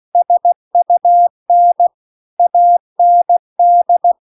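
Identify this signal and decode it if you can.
Morse code, heard as a keyed tone, about 16 wpm, 700 Hz. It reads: SUN AND
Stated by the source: Morse practice text at 16 WPM.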